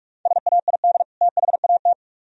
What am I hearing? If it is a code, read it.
Morse code, heard as a keyed tone, about 44 wpm, 700 Hz. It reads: SAID THAT